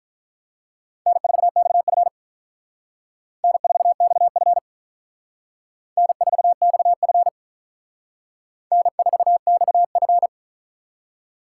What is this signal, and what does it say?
Morse code, heard as a keyed tone, about 45 wpm, 700 Hz. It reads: N4XF N4XF N4XF N4XF